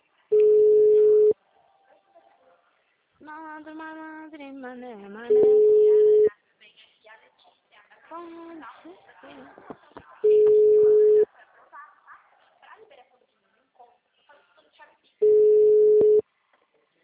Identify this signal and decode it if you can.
Ringback tone (Europe, 425 Hz), with speech.